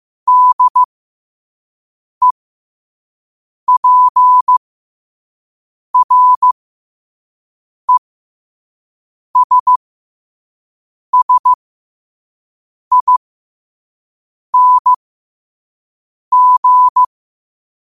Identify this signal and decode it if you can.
Morse code, heard as a keyed tone, about 15 wpm, spaced as for 5 wpm, 1000 Hz. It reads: DEPRESSING
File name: qotdfiles / 5wpm